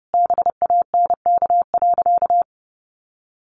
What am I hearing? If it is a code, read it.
Morse code, heard as a keyed tone, about 30 wpm, 700 Hz. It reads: BANK.
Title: Sentences from Top 1400 Words 30wpm